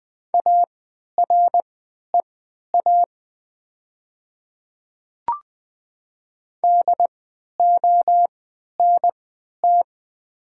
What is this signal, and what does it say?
Morse code, heard as a keyed tone, about 20 wpm, spaced as for 11 wpm, 700 Hz. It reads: AREA DONT
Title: 3x Character Spacing - Top 500 Words - 20wpm